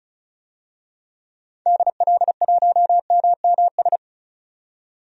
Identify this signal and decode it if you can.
Morse code, heard as a keyed tone, about 36 wpm, 690 Hz. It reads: DL1MMS
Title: DXCC-100 Call Signs - 6 Characters - 35wpm